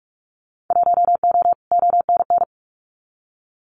Morse code, heard as a keyed tone, about 44 wpm, 700 Hz. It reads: 1O ONN
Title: Winter Field Day Exchanges Encore 45wpm